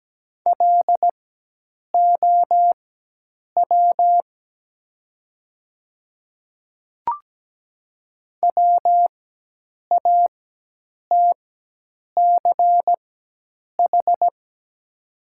Morse code, heard as a keyed tone, about 17 wpm, spaced as for 8 wpm, 700 Hz. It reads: LOW WATCH
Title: 4x Character Spacing - Top 300 Words - 17wpm